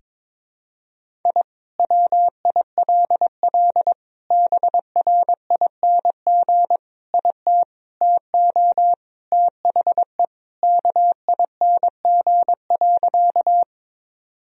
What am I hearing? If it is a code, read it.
Morse code, heard as a keyed tone, about 22 wpm, 700 Hz. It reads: I WILL BRING IT TO THE KING.
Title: Sentences from Top 1000 Words - Review Part 3 - 22wpm